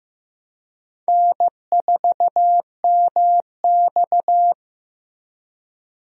Morse code, heard as a keyed tone, about 15 wpm, 700 Hz. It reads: N4MX
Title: USA Call Signs - 4 Characters - Repeated 3x - 15wpm